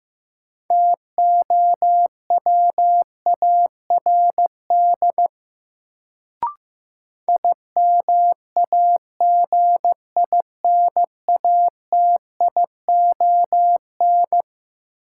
Morse code, heard as a keyed tone, about 15 wpm, 700 Hz. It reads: TOWARD IMAGINATION
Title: Sentences from Top 1300 Words - New Word Drill Alt - 15wpm